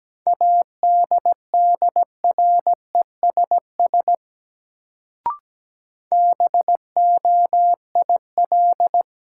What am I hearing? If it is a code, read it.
Morse code, heard as a keyed tone, about 17 wpm, 700 Hz. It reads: ADDRESS BOIL
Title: Sets of 5 Words Encore - New Word Drill - 17wpm